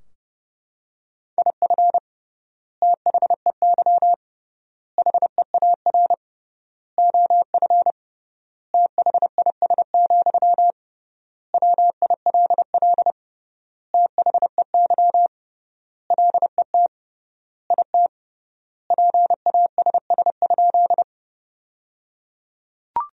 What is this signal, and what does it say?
Morse code, heard as a keyed tone, about 30 wpm, 700 Hz. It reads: IF THEY HEAR OF THIS, WILL THEY LET IT PASS?
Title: 3x Word-Spacing - Sentences from Top 400 Words - 30wpm